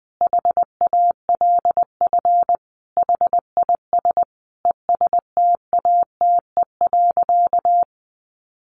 Morse code, heard as a keyed tone, about 20 wpm, 700 Hz. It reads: HALF HIS ESTATE.
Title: Sentences from Top 1200 Words 20wpm